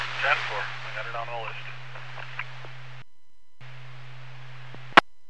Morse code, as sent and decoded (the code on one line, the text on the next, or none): none